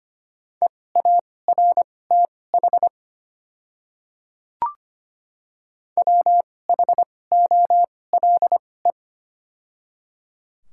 Morse code, heard as a keyed tone, about 25 wpm, spaced as for 18 wpm, 700 Hz. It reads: EARTH WHOLE
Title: 2x Character Spacing - Top 500 Words - 25wpm